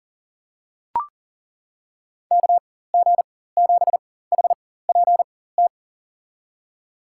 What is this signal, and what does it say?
Morse code, heard as a keyed tone, about 40 wpm, spaced as for 19 wpm, 700 Hz. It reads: KG7HPT
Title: USA Call Signs - 6 Characters - 4x Character Spacing - 40wpm